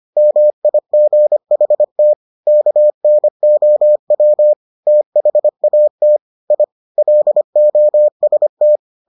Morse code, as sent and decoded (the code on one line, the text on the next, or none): -- .. --. .... - / -.- -. --- .-- / - .... .- - / .. / .-.. --- ... -
MIGHT KNOW THAT I LOST